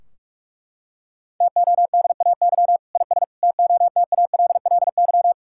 Morse code, heard as a keyed tone, about 41 wpm, 700 Hz. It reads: TODAY IS TOTALLY
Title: Sentences from Top 1600 Words 45wpm